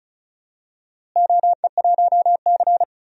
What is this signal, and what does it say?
Morse code, heard as a keyed tone, about 36 wpm, 700 Hz. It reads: OE1C
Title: DXCC-100 Call Signs - 4 Characters - Repeated 2x - 35wpm